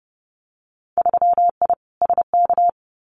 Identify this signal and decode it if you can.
Morse code, heard as a keyed tone, about 31 wpm, 700 Hz. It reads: 3I SK
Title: Winter Field Day Exchanges Encore 30wpm